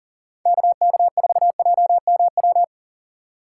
Morse code, heard as a keyed tone, about 40 wpm, 700 Hz. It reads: KK4JMW
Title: USA Call Signs - 6 Characters - 4x Character Spacing - 40wpm